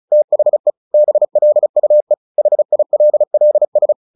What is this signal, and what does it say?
Morse code, heard as a keyed tone, about 34 wpm, 600 Hz. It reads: THE BLUE HILLS
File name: the old man and the sea / the old man and the sea 1 35wpm